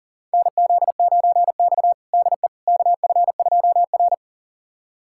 Morse code, heard as a keyed tone, about 40 wpm, 700 Hz. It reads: NZ9X DE KF2R
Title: USA Callsign DE Callsign - 4 Characters - 40wpm